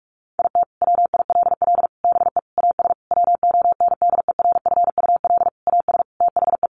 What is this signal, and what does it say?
Morse code, heard as a keyed tone, about 44 wpm, 700 Hz. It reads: IT WILL BE AS WONDERFUL AS THE
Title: Sentences from Top 1400 Words 45wpm